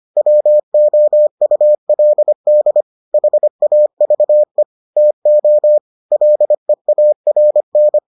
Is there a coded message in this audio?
Yes: Morse code from 0.2 to 8.0 s, heard as a keyed tone, about 25 wpm, 600 Hz. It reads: WOULD HAVE TO LEARN